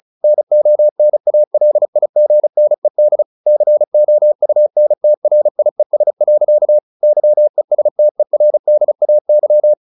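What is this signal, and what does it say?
Morse code, heard as a keyed tone, about 34 wpm, 600 Hz. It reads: NONALIGNED COUNTRIES. YESTERDAY
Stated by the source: Morse practice at 35 wpm from CW med Gnister, sending Ken Follett's novel Never in English.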